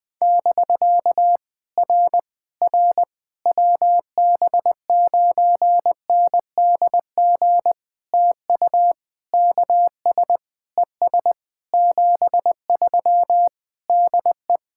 Morse code, heard as a keyed tone, about 19 wpm, 700 Hz. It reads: <BK> R R WB9NDG TU KS ES 73 DE
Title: POTA QSO Practice - 20wpm